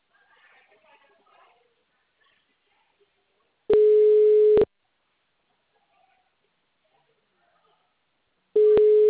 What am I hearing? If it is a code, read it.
Ringback tone (Europe, 425 Hz), with speech.
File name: out- -2023- - - .79967